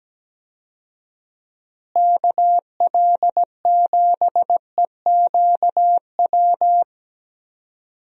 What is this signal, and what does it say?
Morse code, heard as a keyed tone, about 17 wpm, 700 Hz. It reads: KL7EQW
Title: DXCC-100 Call Signs - 6 Characters - Repeated 3x - 17wpm